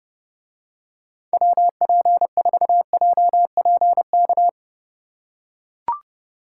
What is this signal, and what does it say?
Morse code, heard as a keyed tone, about 29 wpm, 700 Hz. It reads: WP4JPK